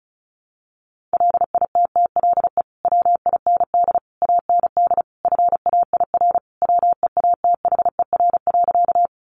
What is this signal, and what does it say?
Morse code, heard as a keyed tone, about 34 wpm, 700 Hz. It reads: LITTLE WIND AND FAIR WEATHER.
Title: Sentences from Top 1300 Words 35wpm